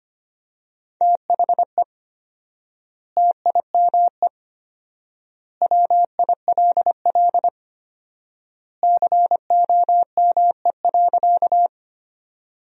Morse code, heard as a keyed tone, about 25 wpm, 700 Hz. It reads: THE TIME WILL COME.
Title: 4x Word-Spacing - Sentences from Top 100 Words - 25wpm